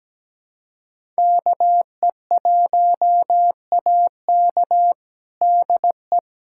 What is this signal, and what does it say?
Morse code, heard as a keyed tone, about 17 wpm, 700 Hz. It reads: KE1AK DE
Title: USA Callsign DE Callsign - 5 Characters - 17wpm